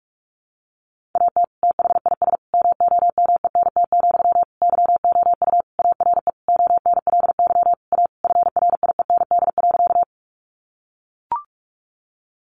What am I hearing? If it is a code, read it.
Morse code, heard as a keyed tone, about 44 wpm, 700 Hz. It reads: AT THIS MOMENT, YOU ARE ONLY A FRIEND.
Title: Sentences from Top 1000 Words 45wpm